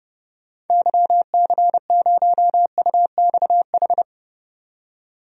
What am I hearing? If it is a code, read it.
Morse code, heard as a keyed tone, about 30 wpm, 700 Hz. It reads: YC0UXH
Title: DXCC-100 Call Signs - 6 Characters - 30wpm